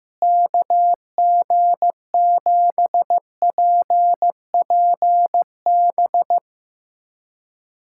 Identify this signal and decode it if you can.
Morse code, heard as a keyed tone, about 15 wpm, 700 Hz. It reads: KG7PPB
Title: USA Callsign DE Callsign 15wpm